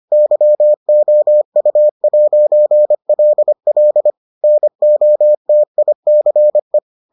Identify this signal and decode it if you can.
Morse code, heard as a keyed tone, about 25 wpm, 600 Hz. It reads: YOU'LL NOTICE